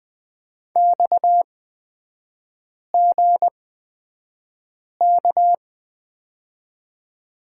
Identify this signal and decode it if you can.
Morse code, heard as a keyed tone, about 20 wpm, spaced as for 5 wpm, 690 Hz. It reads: XGK
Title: 229 Lesson 37e - Letter x - Three Characters 20wpm